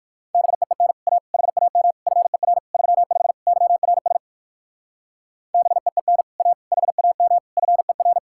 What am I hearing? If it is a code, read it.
Morse code, heard as a keyed tone, about 51 wpm, 700 Hz. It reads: BEEN A HAM FER 35 YRS BEEN A HAM FER
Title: Speed-Racing Common QSO Phrases Encore - 45wpm